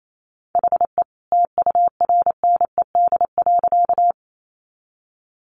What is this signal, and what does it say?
Morse code, heard as a keyed tone, about 28 wpm, 700 Hz. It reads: HE TURNED.
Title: A Sound Of Thunder 28wpm